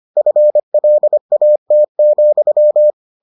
Morse code, heard as a keyed tone, about 25 wpm, 600 Hz. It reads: FLAT,